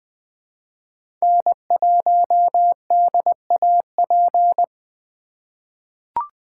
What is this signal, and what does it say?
Morse code, heard as a keyed tone, about 20 wpm, 700 Hz. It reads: N1DAP
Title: USA Call Signs - 3x Character Spacing - 20wpm